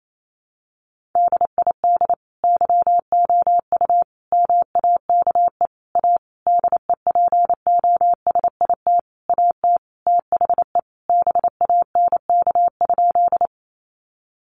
Morse code, heard as a keyed tone, about 28 wpm, 700 Hz. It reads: DID YOU MAKE A DEPOSIT AT THE BANK?
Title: Sentences from Top 1300 Words 28wpm